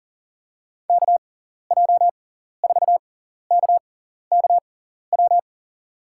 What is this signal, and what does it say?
Morse code, heard as a keyed tone, about 37 wpm, spaced as for 14 wpm, 700 Hz. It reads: KJ4KKW
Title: USA Call Signs - 6 Characters - 6x Character Spacing - 40wpm